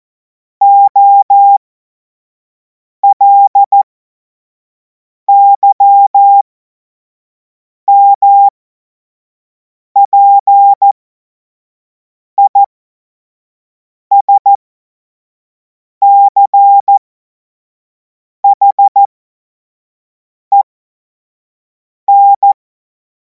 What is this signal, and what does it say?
Morse code, heard as a keyed tone, about 14 wpm, spaced as for 5 wpm, 800 Hz. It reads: OLYMPISCHEN
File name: Morsekurs / Witze / Witz 05. Mai 2026 25 F70